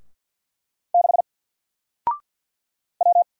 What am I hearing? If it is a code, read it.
Morse code, heard as a keyed tone, about 48 wpm, 700 Hz. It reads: 6 W
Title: Single Letter-Number 50wpm